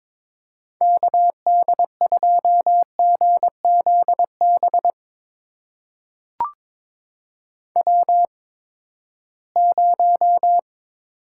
Morse code, heard as a keyed tone, about 22 wpm, spaced as for 15 wpm, 700 Hz. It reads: KD2GZB W0
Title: USA Call Signs - 8x Character Spacing - 22wpm